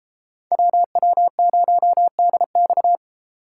Morse code, heard as a keyed tone, about 37 wpm, 700 Hz. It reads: WW0DX